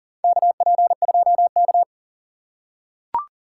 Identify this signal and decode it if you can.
Morse code, heard as a keyed tone, about 40 wpm, 700 Hz. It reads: KP2K
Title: USA Call Signs - 4 Characters - 4x Character Spacing - 40wpm